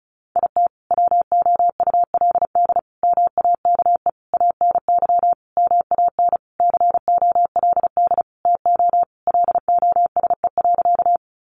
Morse code, heard as a keyed tone, about 34 wpm, 700 Hz. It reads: IT WOULD MAKE ANY MAN COLD TO LOSE.
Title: Sentences from Top 1100 Words 35wpm